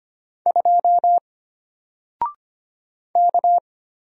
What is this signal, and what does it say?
Morse code, heard as a keyed tone, about 25 wpm, 700 Hz. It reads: 2 K
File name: Single-Letter-Number-25wpm